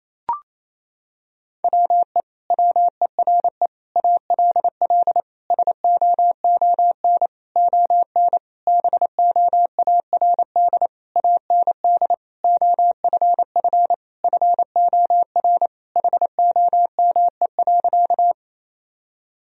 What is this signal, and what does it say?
Morse code, heard as a keyed tone, about 28 wpm, 700 Hz. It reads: WE WERE ALL SOON ON BOARD AND OFF FOR HOME.
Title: Sentences from Top 1000 Words - Review Part 4 - 28wpm